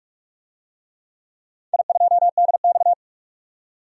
Morse code, heard as a keyed tone, about 44 wpm, 700 Hz. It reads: I2DX
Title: DXCC-100 Call Signs - 4 Characters - 45wpm